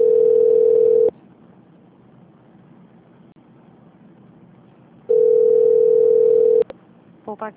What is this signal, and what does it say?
Ringback tone (North America), 440 + 480 Hz.
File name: out- -2039- - - .4393